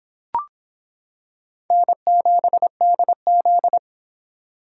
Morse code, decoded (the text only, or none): N7DZ